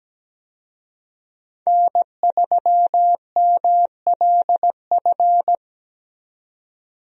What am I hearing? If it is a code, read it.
Morse code, heard as a keyed tone, about 17 wpm, 700 Hz. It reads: N3MLF
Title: USA Call Signs - 5 Characters - 17wpm